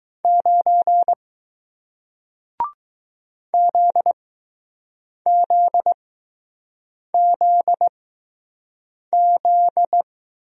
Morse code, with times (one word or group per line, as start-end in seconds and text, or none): 0.2-1.1 9
3.5-10.0 ZZZZ